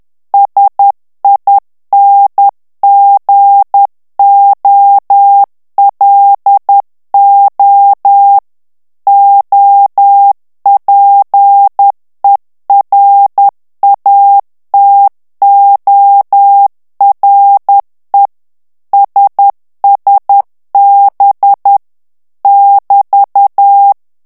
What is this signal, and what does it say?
Morse code, heard as a keyed tone, about 10 wpm, 800 Hz. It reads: SINGOLO OPERATORE SSB <BT>
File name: ITALIEN texte clair 2 vitesse 10 mots minute